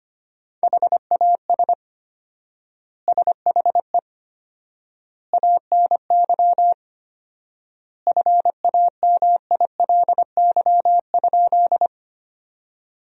Morse code, heard as a keyed tone, about 25 wpm, 700 Hz. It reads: HAS SHE ANY FAMILY?
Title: 4x Word-Spacing - Sentences from Top 400 Words - 25wpm